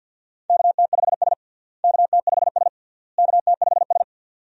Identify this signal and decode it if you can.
Morse code, heard as a keyed tone, about 52 wpm, 700 Hz. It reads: KT5S KT5S KT5S
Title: USA Call Signs - 4 Characters - Repeated 3x - 50wpm